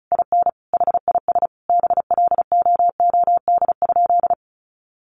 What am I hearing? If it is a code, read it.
Morse code, heard as a keyed tone, about 34 wpm, 700 Hz. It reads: IN HIS BLOOD?
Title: Sentences from Top 1400 Words 35wpm